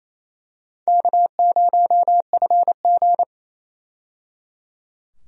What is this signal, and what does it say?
Morse code, heard as a keyed tone, about 28 wpm, 700 Hz. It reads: K0FG
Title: USA Call Signs - 4 Characters - 28wpm